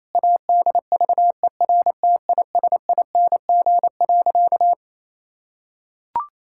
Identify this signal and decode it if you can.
Morse code, heard as a keyed tone, about 28 wpm, 700 Hz. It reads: ADVERTISING.
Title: Sentences from Top 1000 Words - Review Part 6 - 28wpm